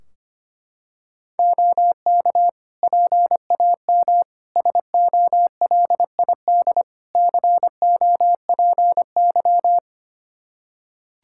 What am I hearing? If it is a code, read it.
Morse code, heard as a keyed tone, about 25 wpm, 700 Hz. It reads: OK PAM SOLID COPY
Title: Practice QSO - Exchange 3 - More Info - 25wpm